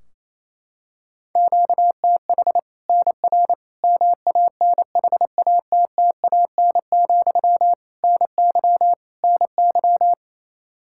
Morse code, heard as a keyed tone, about 28 wpm, 700 Hz. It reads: QTH NR MANHATTAN, NY NY
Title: Practice QSO - Exchange 1 - Beginning - 28wpm